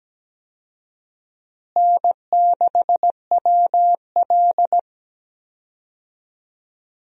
Morse code, heard as a keyed tone, about 17 wpm, 700 Hz. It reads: N6WL
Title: USA Call Signs - 4 Characters - Repeated 3x - 17wpm